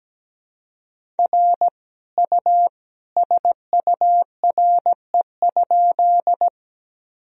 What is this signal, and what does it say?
Morse code, heard as a keyed tone, about 17 wpm, 700 Hz. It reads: R U SURE?
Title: Sentences from Top 300 Words and Abbreviations 17wpm